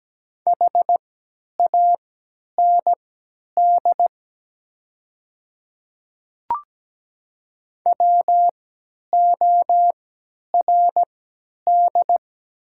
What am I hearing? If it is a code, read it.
Morse code, heard as a keyed tone, about 17 wpm, spaced as for 10 wpm, 700 Hz. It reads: HAND WORD
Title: 3x Character Spacing - Top 200 Words - 17wpm